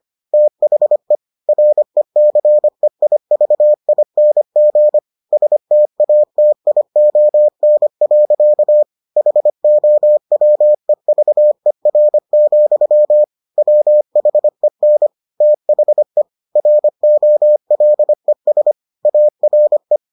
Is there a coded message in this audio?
Yes: Morse code from 0.3 to 20.0 s, heard as a keyed tone, about 25 wpm, 600 Hz. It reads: THE RECEIVING STATION. HOWEVER, WHEN THE ROLES ARE